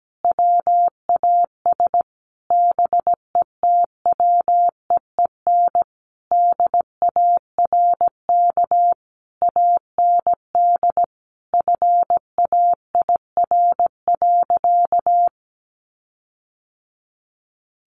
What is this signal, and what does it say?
Morse code, heard as a keyed tone, about 17 wpm, 700 Hz. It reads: WAS BETWEEN DARK AND FAIR.
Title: Sentences from Top 1300 Words 17wpm